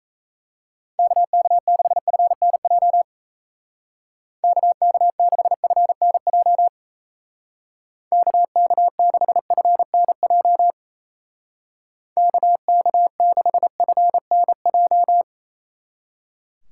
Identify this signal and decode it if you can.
Morse code, heard as a keyed tone, about 34 wpm, 700 Hz. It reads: KK6FNJ KK6FNJ KK6FNJ KK6FNJ
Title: Speed Racing - USA 2x3 Call Signs 28wpm